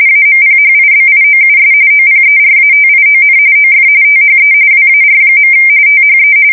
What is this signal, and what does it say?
RTTY, about 45.45 baud. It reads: AMPHIBIOUS OPERATION OF THE WAR. THE
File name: fsk170 / USS-New-Jersey